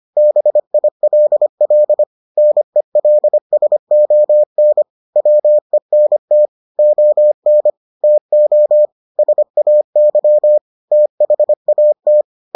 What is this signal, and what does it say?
Morse code, heard as a keyed tone, about 25 wpm, 600 Hz. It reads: BILL NELSON WENT ON TO SAY THAT